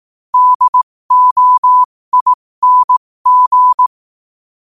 Morse code, heard as a keyed tone, about 18 wpm, spaced as for 15 wpm, 1000 Hz. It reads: DOING
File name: qotdfiles / 15wpm